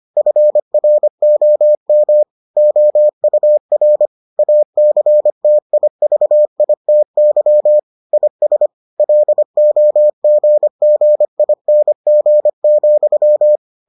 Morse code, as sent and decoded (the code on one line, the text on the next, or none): ..-. .-. --- -- / --- ..- .-. / .- -.-. - .. ...- .. - -.-- / .. ... / .-.. --- --. --. .. -. --. --..--
FROM OUR ACTIVITY IS LOGGING,